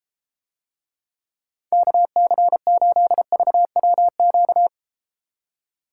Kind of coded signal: Morse code

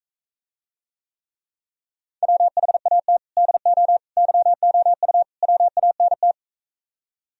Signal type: Morse code

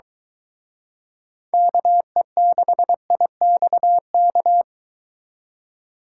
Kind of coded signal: Morse code

